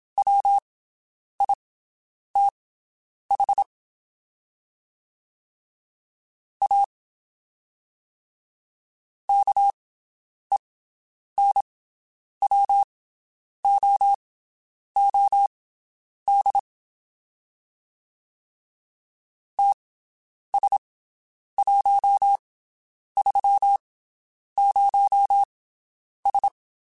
Morse code, heard as a keyed tone, about 27 wpm, spaced as for 9 wpm, 800 Hz. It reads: WITH A KENWOOD TS130S
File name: (spaced) / QSOs Spaced-mp3s / WD0BGP 1 @25 sp